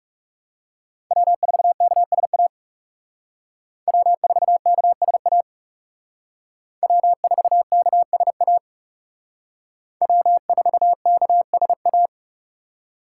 Morse code, heard as a keyed tone, about 38 wpm, 700 Hz. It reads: W4KSA W4KSA W4KSA W4KSA